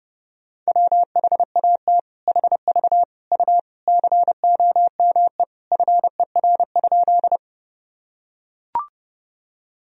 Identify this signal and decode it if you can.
Morse code, heard as a keyed tone, about 32 wpm, 700 Hz. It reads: WHAT HV U COME FER?
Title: Sentences from Top 100 Words and Abbreviations 30wpm